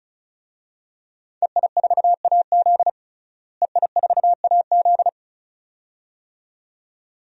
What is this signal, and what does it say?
Morse code, heard as a keyed tone, about 36 wpm, 700 Hz. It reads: EI4AZ EI4AZ
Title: DXCC-100 Call Signs - 5 Characters - Repeated 2x - 35wpm